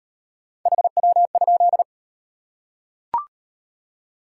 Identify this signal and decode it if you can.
Morse code, heard as a keyed tone, about 38 wpm, 700 Hz. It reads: HW?